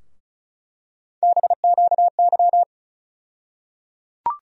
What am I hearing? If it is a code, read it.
Morse code, heard as a keyed tone, about 36 wpm, 700 Hz. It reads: BQY
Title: USA Call Sign Postfixes - 3 Letters 35wpm